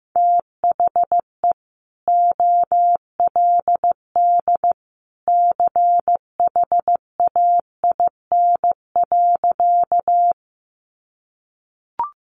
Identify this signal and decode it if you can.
Morse code, heard as a keyed tone, about 15 wpm, 700 Hz. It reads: THE OLD CHAIN.
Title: Sentences from Top 1500 Words 15wpm